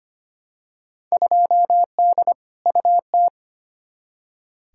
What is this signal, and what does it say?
Morse code, heard as a keyed tone, about 25 wpm, 700 Hz. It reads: 2D UT
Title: Field Day Exchanges Encore - 25wpm